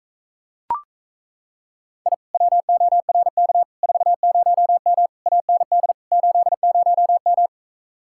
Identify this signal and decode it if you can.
Morse code, heard as a keyed tone, about 37 wpm, 700 Hz. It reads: I WORK 40M AND 80M